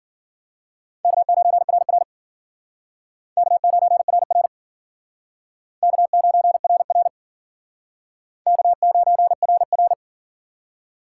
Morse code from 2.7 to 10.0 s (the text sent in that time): K9RR K9RR K9RR